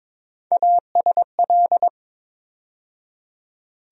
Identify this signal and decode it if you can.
Morse code, heard as a keyed tone, about 22 wpm, 700 Hz. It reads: ASL
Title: USA Call Sign Postfixes - 3 Letters 22wpm